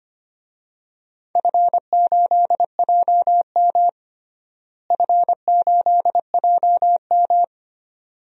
Morse code, heard as a keyed tone, about 25 wpm, 700 Hz. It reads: F8JM F8JM
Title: DXCC-100 Call Signs - 4 Characters - Repeated 2x - 25wpm